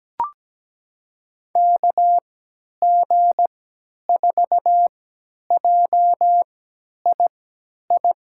Morse code, heard as a keyed tone, about 17 wpm, spaced as for 10 wpm, 700 Hz. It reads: KG4JII
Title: USA Call Signs - 6 Characters - 3x Character Spacing - 17wpm